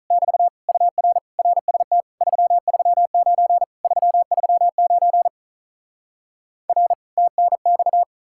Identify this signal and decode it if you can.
Morse code, heard as a keyed tone, about 46 wpm, 700 Hz. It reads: <BT> UR RST 339 339 R TNX